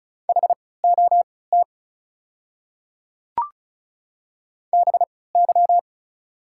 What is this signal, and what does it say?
Morse code, heard as a keyed tone, about 36 wpm, spaced as for 20 wpm, 700 Hz. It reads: HOT BY
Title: 3x Character Spacing - Top 200 Words - 35wpm